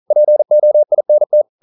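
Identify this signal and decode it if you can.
Morse code, heard as a keyed tone, about 40 wpm, 600 Hz. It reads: POINT
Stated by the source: Morse code practice at 41 wpm, from CW med Gnister.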